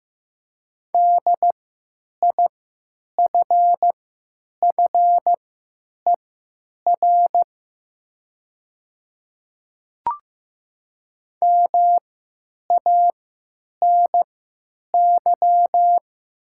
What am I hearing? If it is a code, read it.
Morse code, heard as a keyed tone, about 15 wpm, spaced as for 9 wpm, 700 Hz. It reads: DIFFER MANY
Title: 3x Character Spacing - Top 300 Words - 15wpm